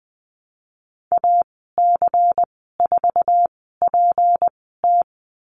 Morse code, heard as a keyed tone, about 20 wpm, spaced as for 14 wpm, 700 Hz. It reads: AC4PT
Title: 161 Lesson 26g - Letter g - Call Signs 20wpm